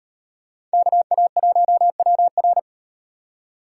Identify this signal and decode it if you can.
Morse code, heard as a keyed tone, about 38 wpm, 700 Hz. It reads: KA1WR